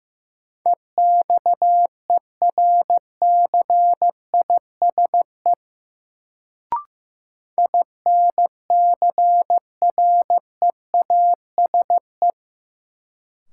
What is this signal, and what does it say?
Morse code, heard as a keyed tone, about 16 wpm, 700 Hz. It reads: EXERCISE INCREASE
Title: Sentences from Top 1000 Words - Review Part 4 - New Word Drill - 15wpm